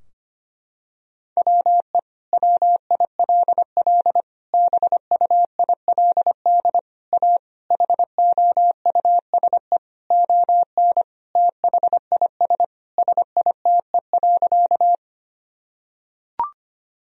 Morse code, heard as a keyed tone, about 25 wpm, 700 Hz. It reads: WE WILL BUILD A HOUSE ON THIS SITE.
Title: Sentences from Top 1000 Words - Review Part 4 - 25wpm